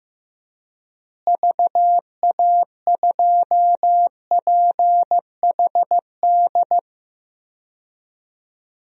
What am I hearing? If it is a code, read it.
Morse code, heard as a keyed tone, about 15 wpm, 700 Hz. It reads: VA2PHD
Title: DXCC-100 Call Signs - 6 Characters - Repeated 2x - 15wpm